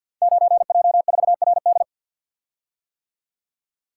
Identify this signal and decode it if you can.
Morse code, heard as a keyed tone, about 52 wpm, 700 Hz. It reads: 9J4RD
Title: Code Groups - 5 Letter:Numbers - 50wpm